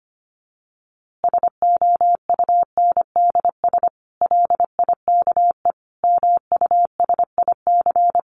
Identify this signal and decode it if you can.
Morse code, heard as a keyed tone, about 25 wpm, 700 Hz. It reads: SOUNDS LIKE MUSIC
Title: A Sound Of Thunder 25wpm